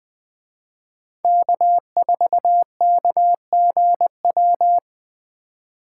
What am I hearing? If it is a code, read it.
Morse code, heard as a keyed tone, about 21 wpm, 700 Hz. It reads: K4KGW
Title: Speed Racing - USA 1x3 Call Signs 17wpm